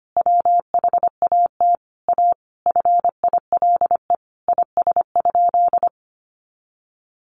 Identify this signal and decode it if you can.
Morse code, heard as a keyed tone, about 25 wpm, 700 Hz. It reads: WHAT A FILE IS?
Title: Sentences from Top 1200 Words 25wpm